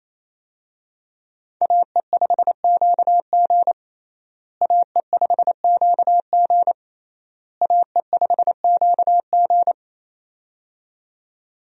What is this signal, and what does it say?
Morse code, heard as a keyed tone, about 28 wpm, 700 Hz. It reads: AE5QG AE5QG AE5QG
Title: USA Call Signs - 5 Characters - Repeated 3x - 28wpm